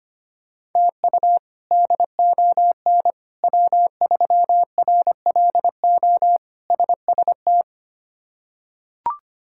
Morse code, heard as a keyed tone, about 25 wpm, 700 Hz. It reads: TU DON W3RLO SST
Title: K1USN SST QSO Practice - 25wpm